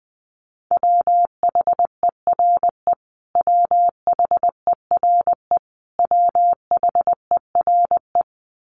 Morse code, heard as a keyed tone, about 20 wpm, 700 Hz. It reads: WHERE WHERE WHERE
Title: Top 500 Words Repeated 20wpm